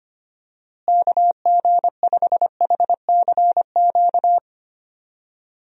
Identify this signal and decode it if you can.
Morse code, heard as a keyed tone, about 25 wpm, 700 Hz. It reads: KG5HCQ
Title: USA Call Signs - 6 Characters - Repeated 2x - 25wpm